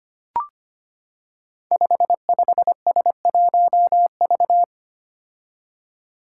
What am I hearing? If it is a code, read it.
Morse code, heard as a keyed tone, about 25 wpm, 700 Hz. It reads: 55S1V
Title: Five Challenging Letter-Number Combinations 25wpm